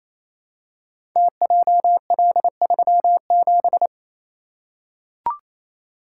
Morse code, heard as a keyed tone, about 29 wpm, 700 Hz. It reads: TJL37
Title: Code Groups - 5 Letter:Numbers - 28wpm